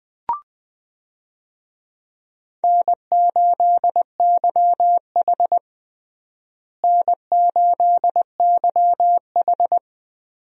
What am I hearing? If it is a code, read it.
Morse code, heard as a keyed tone, about 20 wpm, 700 Hz. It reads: N8YH N8YH
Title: USA Call Signs - 4 Characters - Repeated 2x - 20wpm